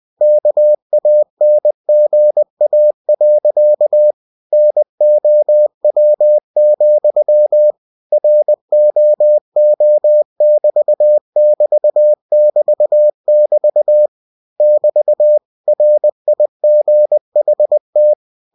Morse code, heard as a keyed tone, about 20 wpm, 600 Hz. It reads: KANGA. NOW, ROO<BT><BT><BT><BT> <BT> RIGHT
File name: cw-books / winnie-the-pooh / winnie-20wpm-page-0069